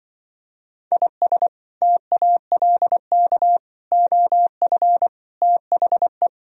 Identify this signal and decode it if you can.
Morse code, heard as a keyed tone, about 24 wpm, 700 Hz. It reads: IS TALK OF THE
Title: Speed-Racing - Sets of 5 Words 20wpm